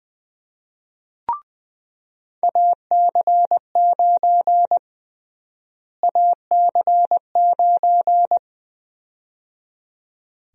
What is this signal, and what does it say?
Morse code, heard as a keyed tone, about 20 wpm, 700 Hz. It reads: AC9 AC9
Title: USA Call Sign Prefixes - Repeated 2x - 20wpm